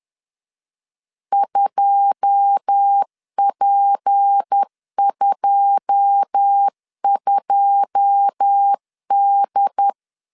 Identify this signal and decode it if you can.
Morse code, heard as a keyed tone, about 10 wpm, 800 Hz. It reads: 2P22D
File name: LEZIONI / 9 2 10wpm 2736